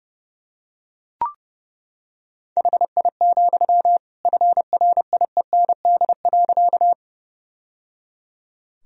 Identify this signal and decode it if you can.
Morse code, heard as a keyed tone, about 30 wpm, 700 Hz. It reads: HI, FRIEND.
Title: Sentences from Top 1000 Words - Review Part 3 - 30wpm